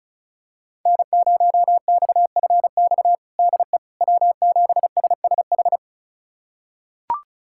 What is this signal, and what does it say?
Morse code, heard as a keyed tone, about 36 wpm, 700 Hz. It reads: N0XFX DE W7SSH
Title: USA Callsign DE Callsign - 5 Characters - 35wpm